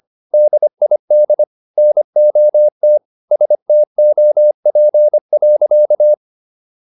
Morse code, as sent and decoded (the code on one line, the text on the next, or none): -.. .. -.. / -. --- - / ... - --- .--. .-.-.-
DID NOT STOP.